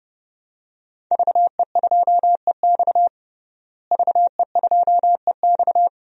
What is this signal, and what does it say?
Morse code, heard as a keyed tone, about 30 wpm, 700 Hz. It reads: VE2EX VE2EX
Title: DXCC-100 Call Signs - 5 Characters - Repeated 2x - 30wpm